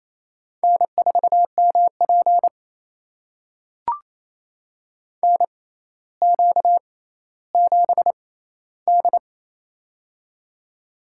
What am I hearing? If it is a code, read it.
Morse code, heard as a keyed tone, about 28 wpm, spaced as for 14 wpm, 700 Hz. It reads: N4MP NQ7D
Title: USA Call Signs - 4 Characters - 6x Character Spacing - 28wpm